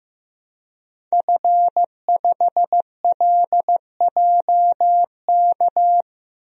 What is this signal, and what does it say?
Morse code, heard as a keyed tone, about 15 wpm, 700 Hz. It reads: F5LJK
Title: DXCC-100 Call Signs - 5 Characters - 15wpm